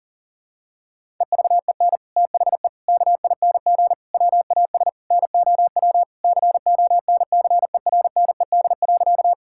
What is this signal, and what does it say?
Morse code, heard as a keyed tone, about 40 wpm, 700 Hz. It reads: EVEN THE KING WAS NOW CONCERNED.
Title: Sentences from Top 1000 Words - Review Part 6 - 40wpm